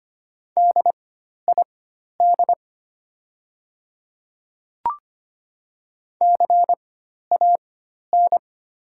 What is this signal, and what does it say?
Morse code, heard as a keyed tone, about 25 wpm, spaced as for 12 wpm, 700 Hz. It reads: DID CAN
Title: 4x Character Spacing - Top 300 Words - 25wpm